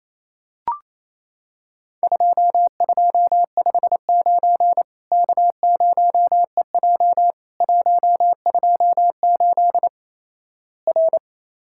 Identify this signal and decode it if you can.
Morse code, heard as a keyed tone, about 28 wpm, 700 Hz. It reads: 2259 K0EJ 128 R
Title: Worked All Europe DX Contest - QTC Exchange - 28wpm